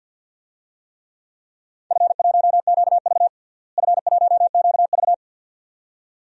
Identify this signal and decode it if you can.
Morse code, heard as a keyed tone, about 53 wpm, 700 Hz. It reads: F1QV F1QV
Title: DXCC-100 Call Signs - 4 Characters - Repeated 2x - 50wpm